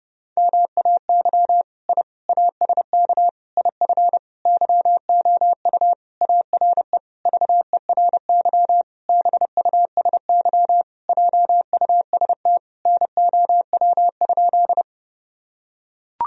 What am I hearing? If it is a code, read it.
Morse code, heard as a keyed tone, about 30 wpm, 700 Hz. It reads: MAY I ASK IF YOU ARE VERY BUSY JUST NOW?
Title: Sentences from Top 1000 Words - Review Part 3 - 30wpm